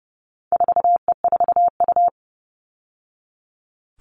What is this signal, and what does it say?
Morse code, heard as a keyed tone, about 30 wpm, 700 Hz. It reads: 4E4U
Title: All Cut Numbers - 4-Digit 30wpm